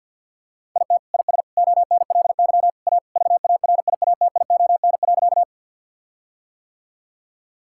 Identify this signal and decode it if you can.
Morse code, heard as a keyed tone, about 50 wpm, 700 Hz. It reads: IT IS ONLY A VARIATION.
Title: Sentences from Top 1600 Words 50wpm